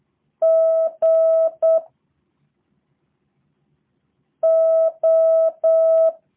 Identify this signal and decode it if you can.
Morse code, heard as a keyed tone, about 8 wpm, 650 Hz. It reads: G O
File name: jh8cbh / 2023cw / review1-4